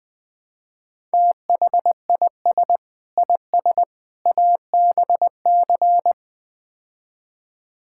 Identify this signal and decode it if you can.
Morse code, heard as a keyed tone, about 20 wpm, 700 Hz. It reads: THIS IS ABC
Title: Build A Phrase - 5-Word Phrases - 20wpm